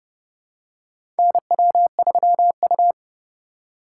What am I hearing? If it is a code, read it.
Morse code, heard as a keyed tone, about 30 wpm, 700 Hz. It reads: NW3U
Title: USA Call Signs - 4 Characters - 4x Character Spacing - 30wpm